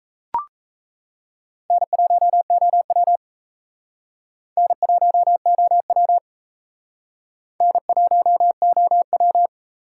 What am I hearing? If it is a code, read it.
Morse code, heard as a keyed tone, about 37 wpm, 700 Hz. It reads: N1OW N1OW N1OW